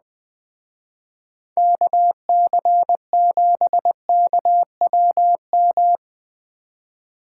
Morse code, heard as a keyed tone, about 21 wpm, 700 Hz. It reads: KC7KWM
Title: Speed Racing - USA 2x3 Call Signs 17wpm